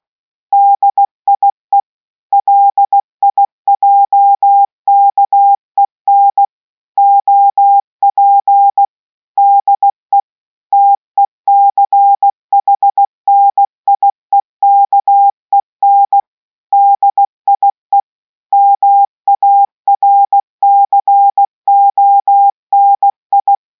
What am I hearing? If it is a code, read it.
Morse code, heard as a keyed tone, about 16 wpm, 800 Hz. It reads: DIE LIJKEN OP DE TECHNIEKEN DIE MARCONI